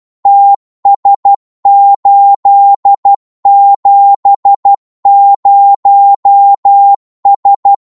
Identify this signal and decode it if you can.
Morse code, heard as a keyed tone, about 12 wpm, 800 Hz. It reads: TS870S
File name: Morse / qso-12wpm